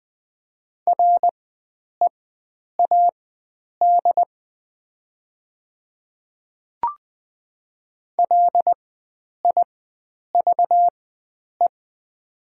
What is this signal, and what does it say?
Morse code, heard as a keyed tone, about 20 wpm, spaced as for 9 wpm, 690 Hz. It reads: READ LIVE
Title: 4x Character Spacing - Top 300 Words - 20wpm